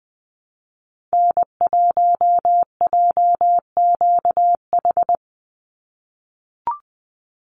Morse code, heard as a keyed tone, about 20 wpm, 700 Hz. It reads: N1JQH
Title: USA Call Signs 20wpm